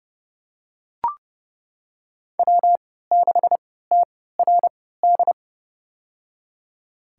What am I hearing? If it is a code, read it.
Morse code, heard as a keyed tone, about 30 wpm, spaced as for 17 wpm, 700 Hz. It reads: W6TRD
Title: USA Call Signs - 5 Characters - 3x Character Spacing - 30wpm